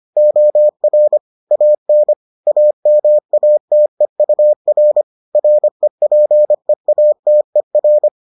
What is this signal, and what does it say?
Morse code, heard as a keyed tone, about 25 wpm, 600 Hz. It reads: OR AN AMATEUR REPEATER